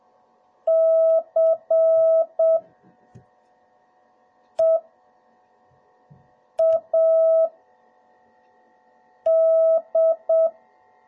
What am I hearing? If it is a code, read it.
Morse code, heard as a keyed tone, about 7 wpm, spaced as for 3 wpm, 660 Hz. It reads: CEAD